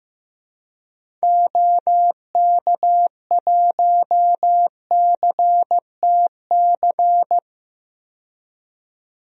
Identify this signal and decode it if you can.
Morse code, heard as a keyed tone, about 16 wpm, 700 Hz. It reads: OK1CTC
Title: DXCC-100 Call Signs - 6 Characters - 15wpm